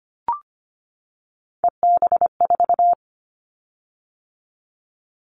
Morse code, heard as a keyed tone, about 25 wpm, 700 Hz. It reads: EB4
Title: Callsign Prefixes for Top 25 DXCC - 25wpm